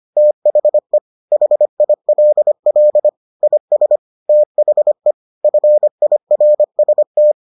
Morse code, heard as a keyed tone, about 25 wpm, 600 Hz. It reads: THE HILL IS THE FIRST